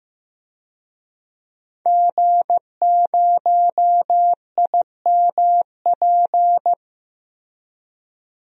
Morse code, heard as a keyed tone, about 15 wpm, 700 Hz. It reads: G0IMP
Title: DXCC-100 Call Signs - 5 Characters - Repeated 2x - 15wpm